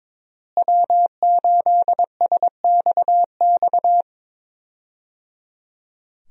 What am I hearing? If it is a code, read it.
Morse code, heard as a keyed tone, about 22 wpm, 700 Hz. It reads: W8SXX
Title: USA Call Signs - 5 Characters - Repeated 3x - 22wpm